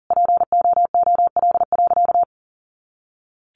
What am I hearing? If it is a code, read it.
Morse code, heard as a keyed tone, about 40 wpm, 700 Hz. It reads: POOL.
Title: Easy Phrases 40wpm